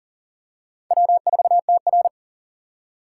Morse code, heard as a keyed tone, about 40 wpm, 700 Hz. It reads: W4TR
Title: USA Call Signs - 4 Characters - 4x Character Spacing - 40wpm